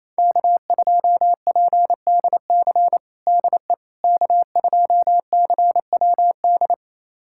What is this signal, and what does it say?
Morse code, heard as a keyed tone, about 28 wpm, 700 Hz. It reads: K2PDC DE K2CWD
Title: USA Callsign DE Callsign - 5 Characters - 28wpm